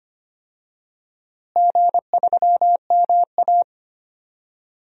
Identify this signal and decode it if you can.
Morse code, heard as a keyed tone, about 25 wpm, 700 Hz. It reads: G3MA
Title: DXCC-100 Call Signs - 4 Characters - Repeated 3x - 25wpm